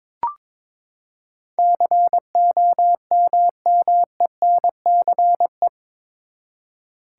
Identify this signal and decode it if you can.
Morse code, heard as a keyed tone, about 21 wpm, 700 Hz. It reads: COMMENCE
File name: practice / 22wpm / Eight Letter Words - 22wpm